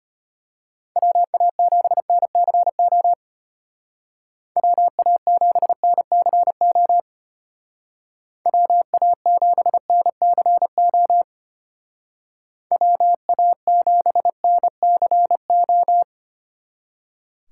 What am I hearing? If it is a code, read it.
Morse code, heard as a keyed tone, about 32 wpm, 700 Hz. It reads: WA7NCO WA7NCO WA7NCO WA7NCO